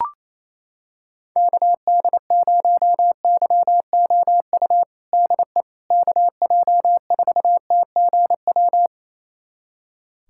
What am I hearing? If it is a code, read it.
Morse code, heard as a keyed tone, about 28 wpm, 700 Hz. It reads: KD0YOU DE KJ4TGW
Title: USA Callsign DE Callsign - 6 Characters - 28wpm